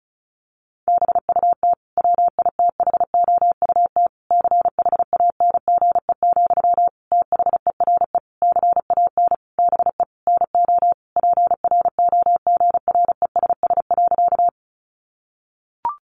Morse code, heard as a keyed tone, about 34 wpm, 700 Hz. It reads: BUT WITHOUT CHANGE, THERE CAN BE NO PROGRESS.
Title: Sentences from Top 1400 Words 35wpm